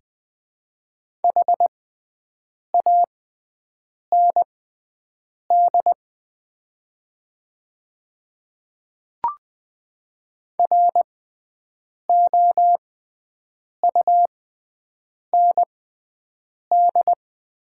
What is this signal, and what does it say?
Morse code, heard as a keyed tone, about 20 wpm, spaced as for 7 wpm, 700 Hz. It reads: HAND ROUND
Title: 6x Character Spacing - Top 500 Words - 20wpm